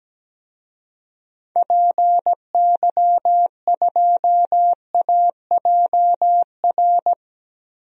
Morse code, heard as a keyed tone, about 17 wpm, 700 Hz. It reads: PY2AJR
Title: DXCC-100 Call Signs - 6 Characters - Repeated 3x - 17wpm